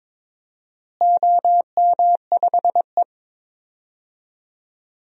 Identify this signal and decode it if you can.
Morse code, heard as a keyed tone, about 22 wpm, 700 Hz. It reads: OM5E
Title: DXCC-100 Call Signs - 4 Characters - Repeated 2x - 22wpm